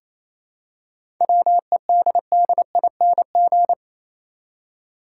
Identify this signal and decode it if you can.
Morse code, heard as a keyed tone, about 28 wpm, 700 Hz. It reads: WEDDING
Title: Seven Letter Words - 28wpm